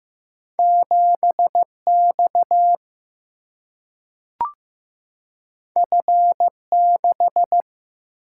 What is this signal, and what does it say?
Morse code, heard as a keyed tone, about 15 wpm, 700 Hz. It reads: 7 X F 6
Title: Single-Letter Single-Number 15wpm